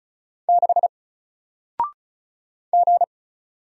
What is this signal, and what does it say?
Morse code, heard as a keyed tone, about 36 wpm, 700 Hz. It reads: 6 G
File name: Single-Letter-Number-35wpm